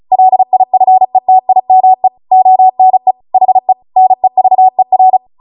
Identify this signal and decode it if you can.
Morse code, heard as a keyed tone, about 36 wpm, 750 Hz. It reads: LIFETIME ONE HE NEVER
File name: Morse / 35WPM